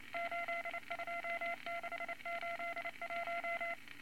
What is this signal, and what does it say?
Morse code, heard as a keyed tone, about 29 wpm, 680 Hz, with noise. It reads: 82681
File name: Morse Code Records / 22 A LCM 172 Twenty Words Per Minute